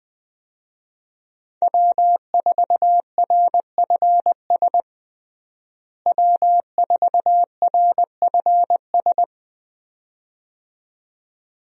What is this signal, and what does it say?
Morse code, heard as a keyed tone, about 20 wpm, 700 Hz. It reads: W4RFS W4RFS
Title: USA Call Signs - 5 Characters - Repeated 2x - 20wpm